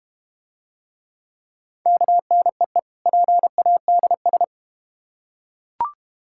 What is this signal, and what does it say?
Morse code, heard as a keyed tone, about 31 wpm, 700 Hz. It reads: KNEE PADS